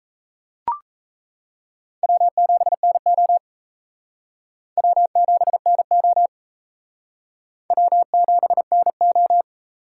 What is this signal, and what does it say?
Morse code, heard as a keyed tone, about 33 wpm, 700 Hz. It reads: W7NO W7NO W7NO
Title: Speed Racing - USA 1x2 Call Signs 28wpm